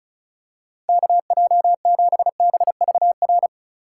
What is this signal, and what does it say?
Morse code, heard as a keyed tone, about 34 wpm, 700 Hz. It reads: KJ7BVR
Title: USA Call Signs - 6 Characters - 2x Character Spacing - 35wpm